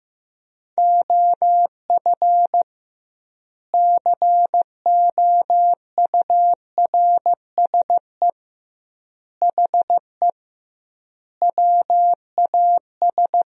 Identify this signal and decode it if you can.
Morse code, heard as a keyed tone, about 14 wpm, 700 Hz. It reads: OF COURSE HE WAS
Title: 2x Word-Spacing - Sentences from Top 500 Words - 15wpm